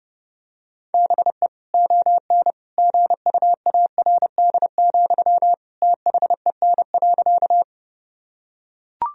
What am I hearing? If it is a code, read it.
Morse code, heard as a keyed tone, about 30 wpm, 700 Hz. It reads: BE ON GUARD, THEN.
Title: Sentences from Top 1000 Words - Review Part 4 - 30wpm